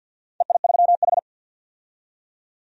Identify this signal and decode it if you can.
Morse code, heard as a keyed tone, about 50 wpm, 700 Hz. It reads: EI3H
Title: DXCC-100 Call Signs - 4 Characters - 50wpm